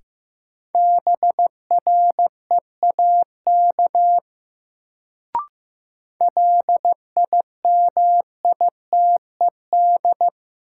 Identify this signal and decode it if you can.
Morse code, heard as a keyed tone, about 15 wpm, 700 Hz. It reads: BREAK LIMITED
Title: Sentences from Top 900 Words - New Word Drill - 15wpm